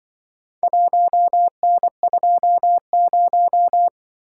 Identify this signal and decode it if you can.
Morse code, heard as a keyed tone, about 23 wpm, 700 Hz. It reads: 1N20
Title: Speed-Racing - Four Letter-Number Combinations - 20wpm